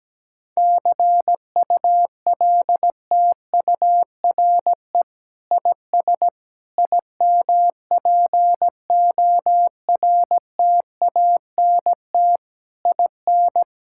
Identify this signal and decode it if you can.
Morse code, heard as a keyed tone, about 17 wpm, 700 Hz. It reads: CULTURE IS IMPORTANT IN
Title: Sentences from Top 1000 Words - Review Part 6 - 17wpm